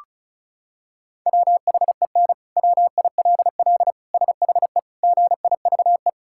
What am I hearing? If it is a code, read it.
Morse code, heard as a keyed tone, about 34 wpm, 700 Hz. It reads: WHEN WILL SHE GIVE
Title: Sentences from Top 1000 Words - Review Part 5 - 35wpm